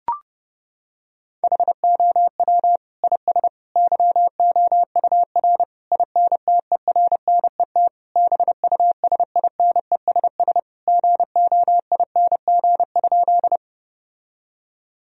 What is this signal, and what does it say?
Morse code, heard as a keyed tone, about 32 wpm, 700 Hz. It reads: HOW IS YOUR INTERNET BUSINESS GOING?
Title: Sentences from Top 1000 Words - Review Part 4 - 30wpm